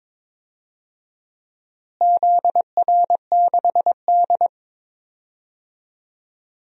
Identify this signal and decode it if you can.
Morse code, heard as a keyed tone, about 23 wpm, 700 Hz. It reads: ZR6D
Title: DXCC-100 Call Signs - 4 Characters - Repeated 2x - 22wpm